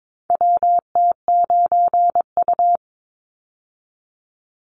Morse code, heard as a keyed tone, about 22 wpm, 700 Hz. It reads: WT9U
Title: Call Signs - 22wpm